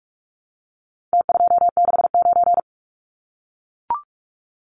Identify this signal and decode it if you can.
Morse code, heard as a keyed tone, about 44 wpm, 700 Hz. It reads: T269
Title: Numbers - Contest Style - 4-Digit 45wpm